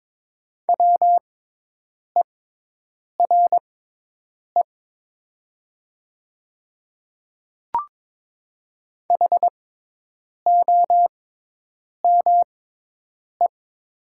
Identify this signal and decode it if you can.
Morse code, heard as a keyed tone, about 22 wpm, spaced as for 8 wpm, 690 Hz. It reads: WERE HOME
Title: 6x Character Spacing - Top 300 Words - 22wpm